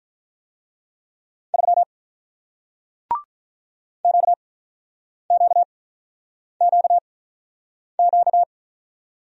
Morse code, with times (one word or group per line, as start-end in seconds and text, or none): 1.5-1.8 3
4.0-8.4 QQQQ